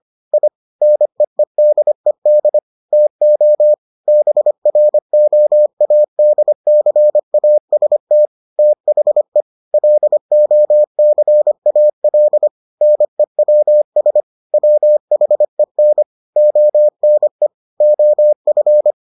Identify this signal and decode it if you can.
Morse code, heard as a keyed tone, about 25 wpm, 600 Hz. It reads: I NEEDED TO BROADCAST THE LOCAL NEWS WHEN ONE OF